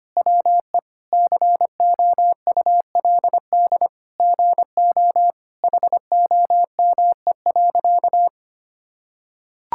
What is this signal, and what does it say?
Morse code, heard as a keyed tone, about 25 wpm, 700 Hz. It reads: WE COULD GO HOME.
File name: practice / 25wpm / Sentences with 2-Word Phrases - 25wpm